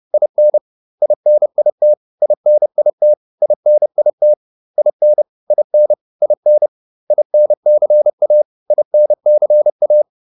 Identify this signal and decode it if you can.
Morse code, heard as a keyed tone, about 30 wpm, 600 Hz. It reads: IN INIT INIT INIT IN IN IN INCA INCA